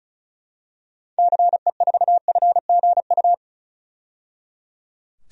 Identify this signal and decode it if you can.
Morse code, heard as a keyed tone, about 36 wpm, 700 Hz. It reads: CE4FGU
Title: DXCC-100 Call Signs - 6 Characters - 35wpm